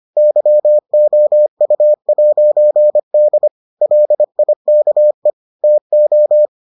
Morse code, heard as a keyed tone, about 25 wpm, 600 Hz. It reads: YOU'D LIKE TO